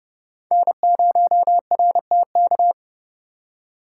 Morse code, heard as a keyed tone, about 28 wpm, 700 Hz. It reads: N0RTK